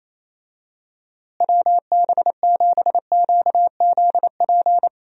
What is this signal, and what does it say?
Morse code, heard as a keyed tone, about 28 wpm, 700 Hz. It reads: WB7QZP
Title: USA Call Signs - 6 Characters - Repeated 2x - 28wpm